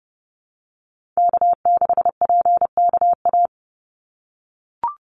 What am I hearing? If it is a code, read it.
Morse code, heard as a keyed tone, about 30 wpm, 700 Hz. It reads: K6PKA
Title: USA Call Signs 30wpm